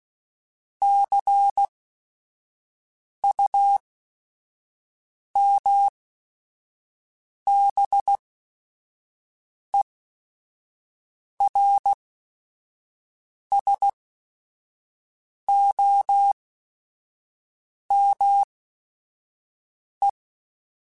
Morse code, heard as a keyed tone, about 16 wpm, spaced as for 5 wpm, 800 Hz. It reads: CUMBERSOME
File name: Technical Subjects / Technical Subjects-mp3s / AND Gate @5